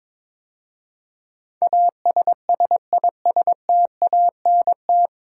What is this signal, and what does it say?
Morse code, heard as a keyed tone, about 22 wpm, 700 Hz. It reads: ASSISTANT
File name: 9 Letter Words - First 3 Letters Given - 22wpm